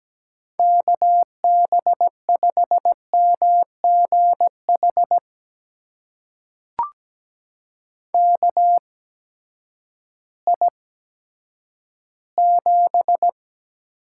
Morse code, heard as a keyed tone, about 17 wpm, spaced as for 10 wpm, 700 Hz. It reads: KB5MGH KI7
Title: USA Call Signs - 6 Characters - 8x Character Spacing - 17wpm